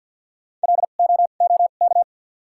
Morse code, heard as a keyed tone, about 53 wpm, 700 Hz. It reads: L O O K
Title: 2x Character Spacing - Top 500 Words - 50wpm